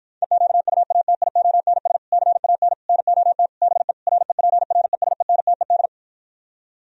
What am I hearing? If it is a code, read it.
Morse code, heard as a keyed tone, about 52 wpm, 700 Hz. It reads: EQUATIONS CAN NOT BE REPRESENTED